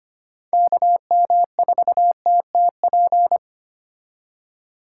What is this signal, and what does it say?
Morse code, heard as a keyed tone, about 32 wpm, 700 Hz. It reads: KM4TTP